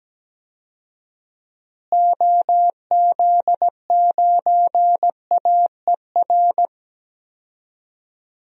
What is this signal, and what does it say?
Morse code, heard as a keyed tone, about 17 wpm, 700 Hz. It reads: OZ9AER
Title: DXCC-100 Call Signs - 6 Characters - Repeated 2x - 17wpm